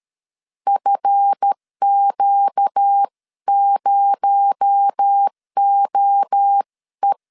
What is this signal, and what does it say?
Morse code, heard as a keyed tone, about 13 wpm, 800 Hz. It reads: FQ0OE
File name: LEZIONI / 7 GF9Z / 7 GF9 12wpm 2319